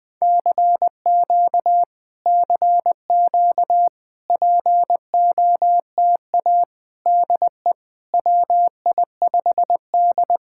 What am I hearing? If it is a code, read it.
Morse code, heard as a keyed tone, about 20 wpm, 700 Hz. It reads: CQ CQ POTA DE WI5D
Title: POTA QSO Practice - 20wpm